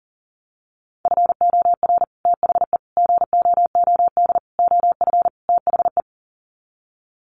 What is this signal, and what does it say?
Morse code, heard as a keyed tone, about 40 wpm, 700 Hz. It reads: FOR THE GOOD OF THE
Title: Sets of 5 Words Encore2 40wpm